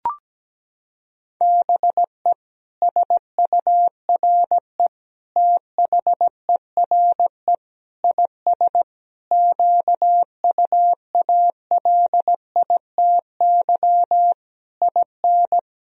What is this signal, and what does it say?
Morse code, heard as a keyed tone, about 17 wpm, 700 Hz. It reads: BE SURE THERE IS QUALITY IN
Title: Sentences from Top 1000 Words - Review Part 4 - 17wpm